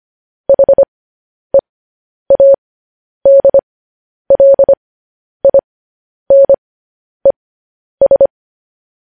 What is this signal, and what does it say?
Morse code, heard as a keyed tone, about 25 wpm, spaced as for 10 wpm, 550 Hz. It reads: HEADLINES
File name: podcast / News headlines-1x-25x10